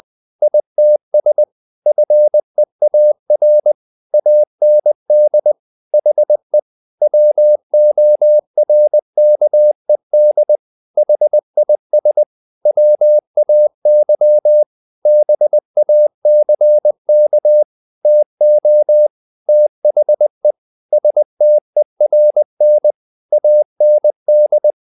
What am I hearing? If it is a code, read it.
Morse code, heard as a keyed tone, about 20 wpm, 600 Hz. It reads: ITS FEAR AND HE WORKED HIS WAY BACK TO THE STERN AND
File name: the old man and the sea part 2 20wpm